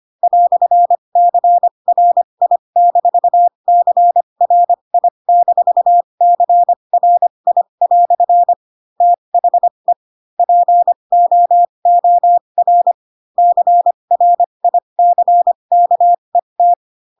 Morse code, heard as a keyed tone, about 25 wpm, spaced as for 22 wpm, 700 Hz. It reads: "CRI-CRI-CRI" THE POOR CRICKET
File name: Pinocchio / Pinocchio ch4 18 25 ik1hge